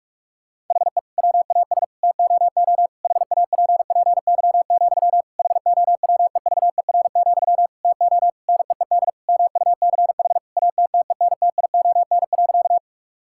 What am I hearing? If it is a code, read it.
Morse code, heard as a keyed tone, about 46 wpm, 700 Hz. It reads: HE WAS TOO HAPPY, HOWEVER, TO NEED MUCH ATTENTION.
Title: Sentences from Top 1000 Words - Review Part 6 - 45wpm